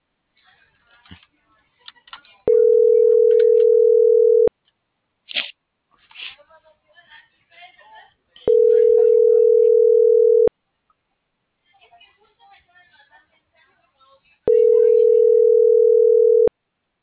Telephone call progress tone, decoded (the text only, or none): ringback tone